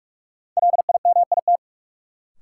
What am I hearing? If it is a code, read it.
Morse code, heard as a keyed tone, about 46 wpm, 700 Hz. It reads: LIMIT
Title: Sentences from Top 1500 Words - New Word Drill Alt - 45wpm